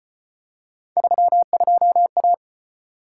Morse code, heard as a keyed tone, about 35 wpm, 700 Hz. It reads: 32A